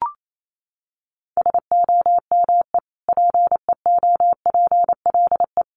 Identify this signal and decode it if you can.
Morse code, heard as a keyed tone, about 28 wpm, 700 Hz. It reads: SOME PEOPLE
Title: Sentences from Top 1500 Words 28wpm